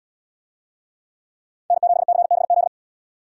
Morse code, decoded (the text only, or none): N7CBZ